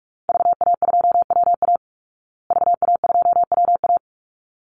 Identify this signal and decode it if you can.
Morse code, heard as a keyed tone, about 44 wpm, 700 Hz. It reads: VA2WA VA2WA
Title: Call Signs Repeated 3x - 45wpm